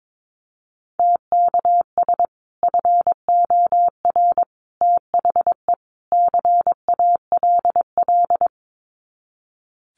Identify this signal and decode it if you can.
Morse code, heard as a keyed tone, about 22 wpm, 700 Hz. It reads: TKS FOR THE CALL
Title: Common QSO Phrases 22wpm